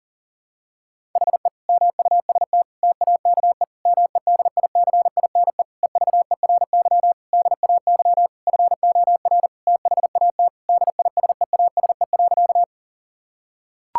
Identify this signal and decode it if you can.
Morse code, heard as a keyed tone, about 40 wpm, 700 Hz. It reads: HE MUST TAKE MEDICINE EVERY DAY FOR THAT DISEASE.
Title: Sentences from Top 1000 Words - Review Part 6 - 40wpm